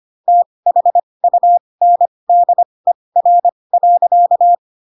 Morse code, heard as a keyed tone, about 25 wpm, spaced as for 20 wpm, 700 Hz. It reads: THUNDER.